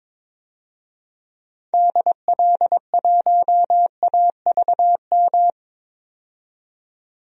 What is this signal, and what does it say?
Morse code, heard as a keyed tone, about 22 wpm, 700 Hz. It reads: DL1AVM
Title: DXCC-100 Call Signs - 6 Characters - 22wpm